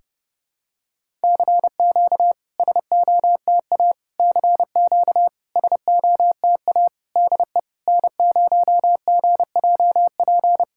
Morse code, heard as a keyed tone, about 30 wpm, 700 Hz. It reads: CQ SOTA CQ SOTA DE N0GJP
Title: SOTA QSO Practice 30wpm